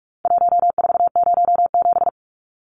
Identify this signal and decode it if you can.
Morse code, heard as a keyed tone, about 44 wpm, 700 Hz. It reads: 1407
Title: Numbers - Contest Style - 4-Digit 45wpm